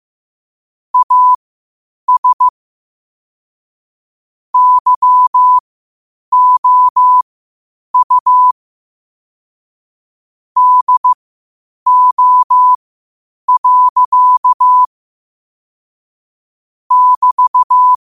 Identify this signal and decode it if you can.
Morse code, heard as a keyed tone, about 15 wpm, spaced as for 8 wpm, 1000 Hz. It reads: AS YOU DO. <BT>
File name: qotdfiles / 8wpm